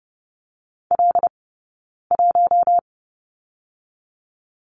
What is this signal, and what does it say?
Morse code, heard as a keyed tone, about 30 wpm, 700 Hz. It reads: L 1
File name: Lesson 20d - Number 3 - Two Characters 30wpm